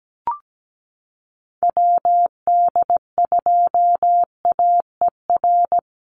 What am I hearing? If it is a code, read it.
Morse code, heard as a keyed tone, about 17 wpm, 700 Hz. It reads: WD2AER
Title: USA Call Signs Repeated 3x 17wpm